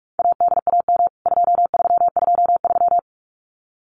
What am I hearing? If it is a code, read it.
Morse code, heard as a keyed tone, about 44 wpm, 700 Hz. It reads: ADAM 2323
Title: CWT Exchange 45wpm